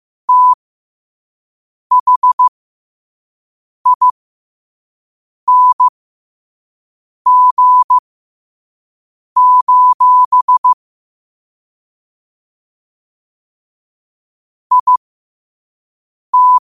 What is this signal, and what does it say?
Morse code, heard as a keyed tone, about 15 wpm, spaced as for 5 wpm, 1000 Hz. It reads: THING: IT
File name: qotdfiles / 5wpm